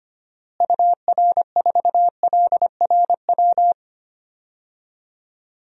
Morse code, heard as a keyed tone, about 25 wpm, 700 Hz. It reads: UR4LRW
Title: DXCC-100 Call Signs - 6 Characters - Repeated 3x - 25wpm